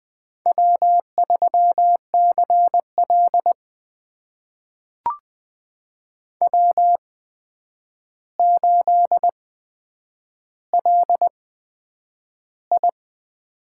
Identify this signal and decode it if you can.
Morse code, heard as a keyed tone, about 20 wpm, spaced as for 9 wpm, 700 Hz. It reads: W3CL W8LI
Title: USA Call Signs - 4 Characters - 8x Character Spacing - 20wpm